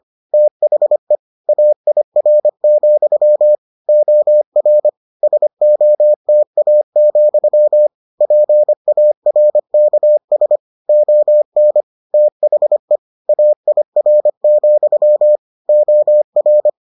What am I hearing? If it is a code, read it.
Morse code, heard as a keyed tone, about 25 wpm, 600 Hz. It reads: THE AIR, OR SOTA, PARKS ON THE AIR, OR